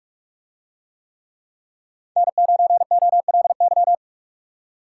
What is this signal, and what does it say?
Morse code, heard as a keyed tone, about 42 wpm, 700 Hz. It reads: N9OLY